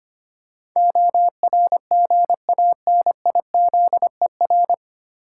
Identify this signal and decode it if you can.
Morse code, heard as a keyed tone, about 25 wpm, 700 Hz. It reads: ORGANIZER
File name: Nine Letter Words - 25wpm